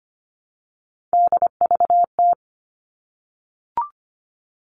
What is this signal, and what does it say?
Morse code, heard as a keyed tone, about 25 wpm, 700 Hz. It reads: DVT
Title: All Cut Numbers - 3-Digit 25wpm